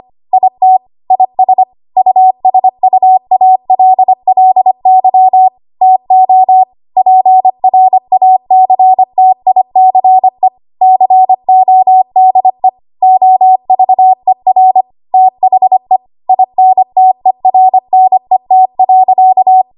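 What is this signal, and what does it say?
Morse code, heard as a keyed tone, about 25 wpm, 750 Hz. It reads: IT IS USUALLY TO PRACTICE CODE OVER THE INTERNET.
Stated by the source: ARRL 25 WPM code practice archive.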